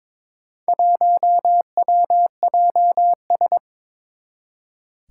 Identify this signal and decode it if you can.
Morse code, heard as a keyed tone, about 22 wpm, 700 Hz. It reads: 1WJS
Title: Four Challenging Letter-Number Combinations 22wpm